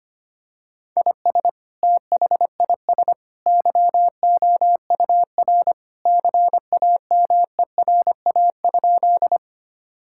Morse code, heard as a keyed tone, about 25 wpm, 700 Hz. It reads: IS THIS YOUR CAMERA?
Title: Sentences from Top 1000 Words - Review Part 5 - 25wpm